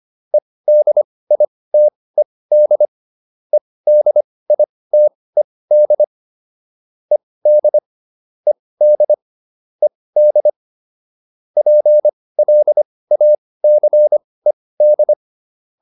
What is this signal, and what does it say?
Morse code, heard as a keyed tone, about 25 wpm, spaced as for 18 wpm, 600 Hz. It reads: EDITED EDITED ED ED ED PLACED